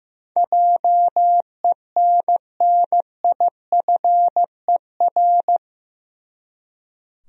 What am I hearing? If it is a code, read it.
Morse code, heard as a keyed tone, about 15 wpm, 700 Hz. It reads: JENNIFER
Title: USA Common Names - 15wpm